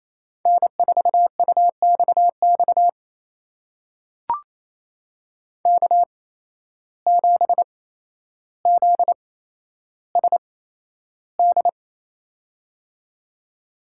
Morse code, heard as a keyed tone, about 28 wpm, spaced as for 12 wpm, 700 Hz. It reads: N4UXX K7ZSD
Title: USA Call Signs - 5 Characters - 8x Character Spacing - 28wpm